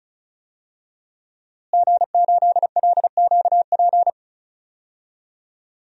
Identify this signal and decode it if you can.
Morse code, heard as a keyed tone, about 34 wpm, 700 Hz. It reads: G8LQP
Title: DXCC-100 Call Signs - 5 Characters - Repeated 2x - 35wpm